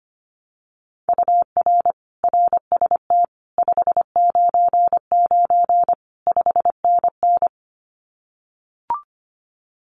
Morse code, heard as a keyed tone, about 25 wpm, 700 Hz. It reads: UR RST 599 5NN
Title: Common QSO Phrases Encore 25wpm